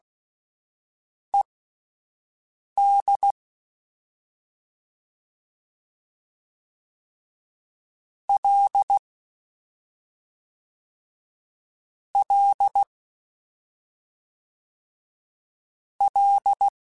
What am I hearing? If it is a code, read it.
Morse code, heard as a keyed tone, about 16 wpm, spaced as for 5 wpm, 800 Hz. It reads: ED L L L
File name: Ls / Single L m3s / Single L Prefixe 3x2 Twice @5